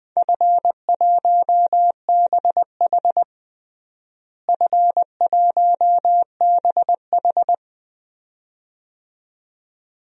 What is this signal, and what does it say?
Morse code, heard as a keyed tone, about 21 wpm, 700 Hz. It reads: F1BH F1BH
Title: DXCC-100 Call Signs - 4 Characters - Repeated 2x - 20wpm